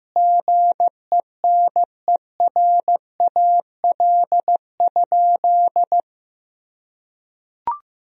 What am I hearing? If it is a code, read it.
Morse code, heard as a keyed tone, about 15 wpm, 700 Hz. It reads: GENERAL?
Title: Sentences from Top 1000 Words - Review Part 3 - 15wpm